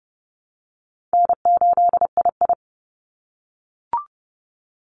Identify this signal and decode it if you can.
Morse code, heard as a keyed tone, about 30 wpm, 700 Hz. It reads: N8II
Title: Call Signs - 30wpm